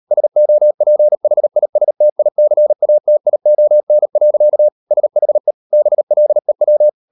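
Morse code, heard as a keyed tone, about 38 wpm, 600 Hz. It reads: SOPHISTICATION. SHE BLEW
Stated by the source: Morse code practice at 38 WPM, sending Ken Follett's novel Never in English, from CW med Gnister.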